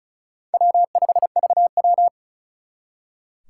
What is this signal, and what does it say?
Morse code, heard as a keyed tone, about 45 wpm, 700 Hz. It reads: W5VW